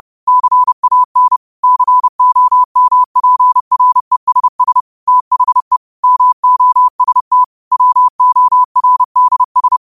Morse code, heard as a keyed tone, about 30 wpm, 1000 Hz. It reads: CAN COMPRESS THE MOST WORDS